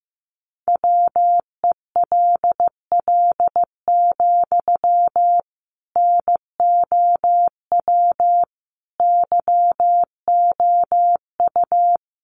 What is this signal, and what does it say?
Morse code, heard as a keyed tone, about 15 wpm, 700 Hz. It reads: WELL, NOW YOU
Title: Sentences from Top 600 Words 15wpm